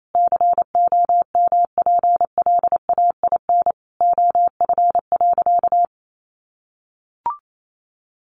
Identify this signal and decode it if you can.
Morse code, heard as a keyed tone, about 28 wpm, 700 Hz. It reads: COMPLAIN OF.
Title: Sentences from Top 1300 Words 28wpm